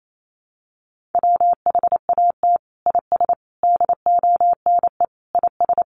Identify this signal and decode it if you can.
Morse code, heard as a keyed tone, about 28 wpm, 700 Hz. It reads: WHAT IS DONE IS
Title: Sentences from Top 500 Words 28wpm